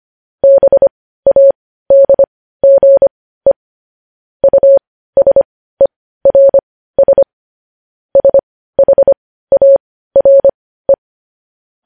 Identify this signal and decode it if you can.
Morse code, heard as a keyed tone, about 25 wpm, spaced as for 15 wpm, 550 Hz. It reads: BADGE USERS SHARE
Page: News Headlines in Morse Code at 15 WPM